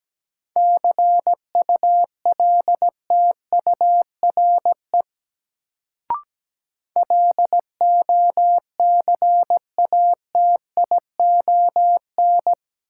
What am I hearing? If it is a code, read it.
Morse code, heard as a keyed tone, about 17 wpm, 700 Hz. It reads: CULTURE LOCATION
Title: Sentences from Top 1000 Words - Review Part 6 - New Word Drill - 17wpm